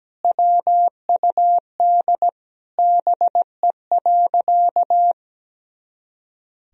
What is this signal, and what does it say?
Morse code, heard as a keyed tone, about 17 wpm, 700 Hz. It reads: WUD BE.
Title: Sentences from Top 200 Words and Abbreviations 17wpm